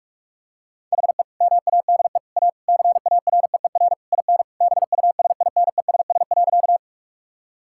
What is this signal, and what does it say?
Morse code, heard as a keyed tone, about 46 wpm, 700 Hz. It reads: HE MADE A CAREER IN BUSINESS.
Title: Sentences from Top 1000 Words - Review Part 6 - 45wpm